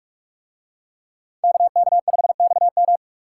Morse code, heard as a keyed tone, about 46 wpm, 700 Hz. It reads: KK5XM